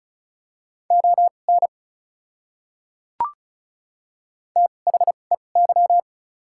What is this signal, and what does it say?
Morse code, heard as a keyed tone, about 36 wpm, spaced as for 25 wpm, 700 Hz. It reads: ON THEY
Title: 2x Character Spacing - Top 200 Words - 35wpm